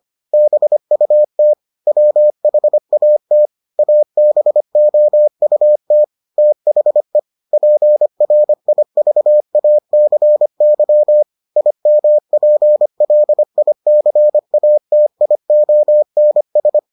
Morse code, heard as a keyed tone, about 25 wpm, 600 Hz. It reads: BUT WHAT ABOUT THE PRIVACY IMPLICATIONS